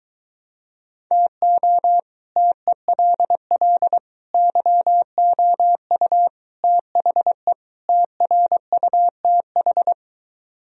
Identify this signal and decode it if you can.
Morse code, heard as a keyed tone, about 23 wpm, 700 Hz. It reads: TO TELL YOU THE TRUTH